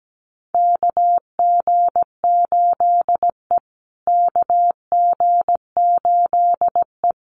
Morse code, heard as a keyed tone, about 17 wpm, 700 Hz. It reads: KG8E KG8E
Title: USA Call Signs Repeated 3x 17wpm